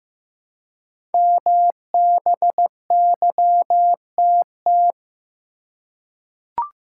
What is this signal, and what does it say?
Morse code, heard as a keyed tone, about 15 wpm, 700 Hz. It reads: MBYTT
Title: Code Groups - 5 Letter:Numbers - 15wpm